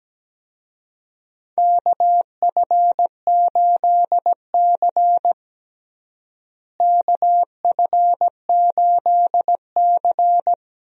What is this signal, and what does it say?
Morse code, heard as a keyed tone, about 17 wpm, 700 Hz. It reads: KF8C KF8C
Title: USA Call Signs - 4 Characters - Repeated 3x - 17wpm